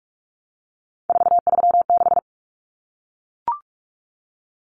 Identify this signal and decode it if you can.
Morse code, heard as a keyed tone, about 44 wpm, 700 Hz. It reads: 436
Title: Common Cut Numbers - 3-Digit 45wpm